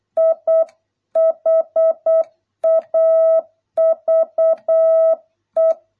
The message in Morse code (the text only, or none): I HAVE